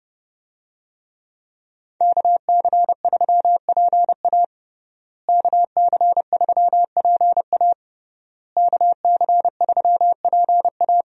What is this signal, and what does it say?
Morse code, heard as a keyed tone, about 32 wpm, 700 Hz. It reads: KC3PA KC3PA KC3PA
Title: USA Call Signs - 5 Characters - Repeated 3x - 30wpm